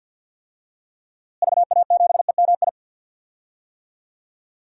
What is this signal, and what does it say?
Morse code, heard as a keyed tone, about 47 wpm, 700 Hz. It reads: VA7EMI
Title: DXCC-100 Call Signs - 6 Characters - Repeated 2x - 50wpm